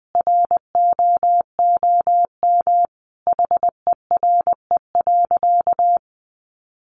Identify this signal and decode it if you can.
Morse code, heard as a keyed tone, about 20 wpm, 700 Hz. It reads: ROOM HERE.
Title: Sentences from Top 1300 Words 20wpm